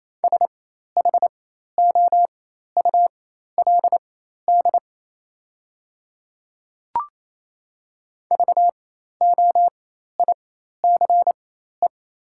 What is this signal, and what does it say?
Morse code, heard as a keyed tone, about 28 wpm, spaced as for 13 wpm, 700 Hz. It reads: SHOULD VOICE
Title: 4x Character Spacing - Top 500 Words - 28wpm